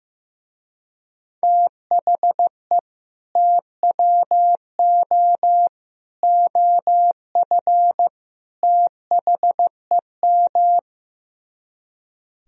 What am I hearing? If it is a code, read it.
Morse code, heard as a keyed tone, about 15 wpm, 700 Hz. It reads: THE TWO OF THEM
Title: Speed-Racing - Sets of 4 Words 15wpm